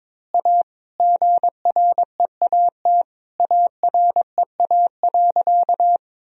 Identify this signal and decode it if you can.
Morse code, heard as a keyed tone, about 22 wpm, 700 Hz. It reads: A GREAT AREA.
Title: Sentences from Top 1000 Words - Review Part 2 - 22wpm